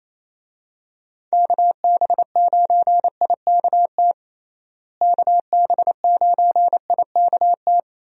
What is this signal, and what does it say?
Morse code, heard as a keyed tone, about 28 wpm, 700 Hz. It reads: KB9IKT KB9IKT
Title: USA Call Signs - 6 Characters - Repeated 3x - 28wpm